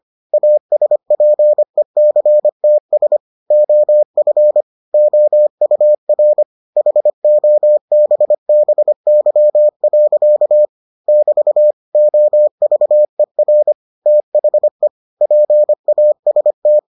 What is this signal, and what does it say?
Morse code, heard as a keyed tone, about 25 wpm, 600 Hz. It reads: ASPECTS OF OUR HOBBY. <BT> OVER THE PAST